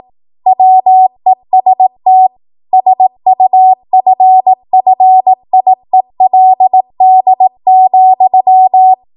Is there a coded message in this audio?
Yes: Morse code from 0.5 to 8.9 s, heard as a keyed tone, about 18 wpm, 750 Hz. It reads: WEST SUFFIELD,